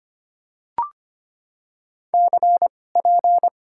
Morse code, heard as a keyed tone, about 25 wpm, spaced as for 18 wpm, 700 Hz. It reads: CP
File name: Top 90 QSO Elements - 2x Character Spacing - 25wpm